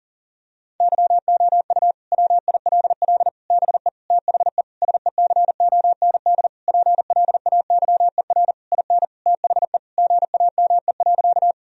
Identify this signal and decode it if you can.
Morse code, heard as a keyed tone, about 40 wpm, 700 Hz. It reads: YOU WILL BE THE SECOND PLAYER IN THE GAME.
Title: Sentences from Top 1000 Words - Review Part 4 - 40wpm